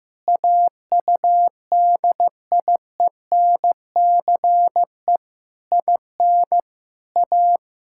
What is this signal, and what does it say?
Morse code, heard as a keyed tone, about 15 wpm, 700 Hz. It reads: AUDIENCE IN A
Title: Sentences from Top 1000 Words - Review Part 5 - 15wpm